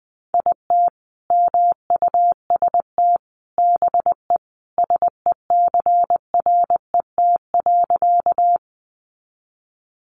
Morse code, heard as a keyed tone, about 20 wpm, 700 Hz. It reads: IT MUST BE SECRET.
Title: Sentences from Top 1200 Words 20wpm